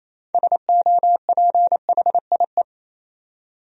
Morse code, heard as a keyed tone, about 28 wpm, 700 Hz. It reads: SOPHIE
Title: USA Common Names - 28wpm